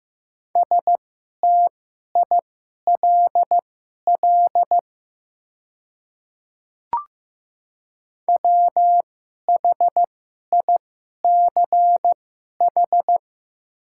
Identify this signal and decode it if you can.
Morse code, heard as a keyed tone, about 15 wpm, spaced as for 11 wpm, 700 Hz. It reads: STILL WHICH
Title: 2x Character Spacing - Top 500 Words - 15wpm